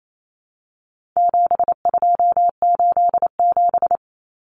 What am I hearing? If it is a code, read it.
Morse code, heard as a keyed tone, about 28 wpm, 700 Hz. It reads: 7287
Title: Common Cut Numbers - 4-Digit 28wpm